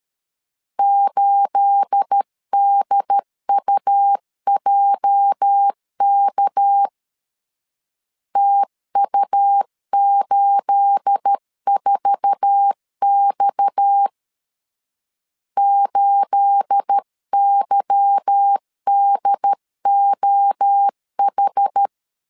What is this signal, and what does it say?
Morse code, heard as a keyed tone, about 13 wpm, 800 Hz. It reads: 8DUJK TU84X 8YDOH
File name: LEZIONI / 9 28 12wpm 2351